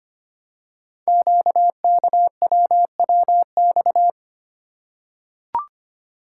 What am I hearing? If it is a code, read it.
Morse code, heard as a keyed tone, about 25 wpm, 700 Hz. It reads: QKWWX
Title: Code Groups - 5 Letters - 25wpm